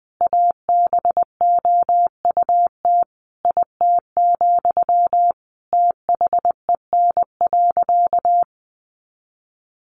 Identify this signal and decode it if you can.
Morse code, heard as a keyed tone, about 20 wpm, 700 Hz. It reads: ABOUT IT, THEN.
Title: Sentences from Top 200 Words 20wpm